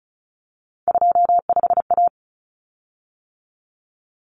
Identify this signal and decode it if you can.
Morse code, heard as a keyed tone, about 34 wpm, 700 Hz. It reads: 25A
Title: Common Cut Numbers - 3-Digit 35wpm